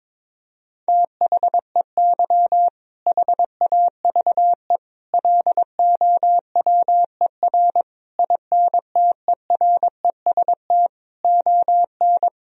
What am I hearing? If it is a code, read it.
Morse code, heard as a keyed tone, about 22 wpm, 700 Hz. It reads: THEY HAVE LOWER INTEREST ON
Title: Sentences from Top 1000 Words - Review Part 4 - 22wpm